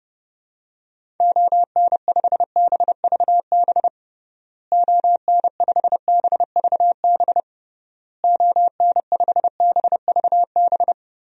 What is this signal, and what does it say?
Morse code, heard as a keyed tone, about 30 wpm, 700 Hz. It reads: ON5BVB ON5BVB ON5BVB
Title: DXCC-100 Call Signs - 6 Characters - Repeated 3x - 30wpm